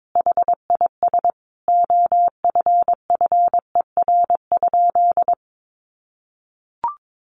Morse code, heard as a keyed tone, about 22 wpm, 700 Hz. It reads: HIS OFFER?
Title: Sentences from Top 1300 Words 22wpm